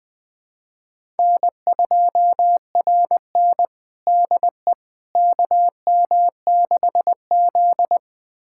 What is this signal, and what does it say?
Morse code, heard as a keyed tone, about 21 wpm, 700 Hz. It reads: N2RN DE KM6Z
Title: USA Callsign DE Callsign - 4 Characters - 20wpm